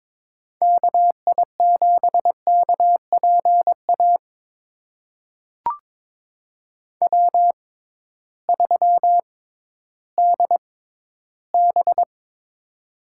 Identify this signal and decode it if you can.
Morse code, heard as a keyed tone, about 22 wpm, spaced as for 13 wpm, 700 Hz. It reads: KI7KPA W3DB
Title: USA Call Signs - 6x Character Spacing - 22wpm